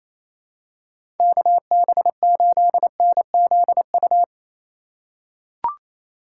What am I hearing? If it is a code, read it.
Morse code, heard as a keyed tone, about 28 wpm, 700 Hz. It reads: KB8NZU
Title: USA Call Signs - 6 Characters - 4x Character Spacing - 28wpm